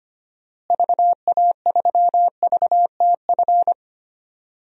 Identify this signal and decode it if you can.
Morse code, heard as a keyed tone, about 25 wpm, 700 Hz. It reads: VA3VTF
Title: DXCC-100 Call Signs - 6 Characters - Repeated 2x - 25wpm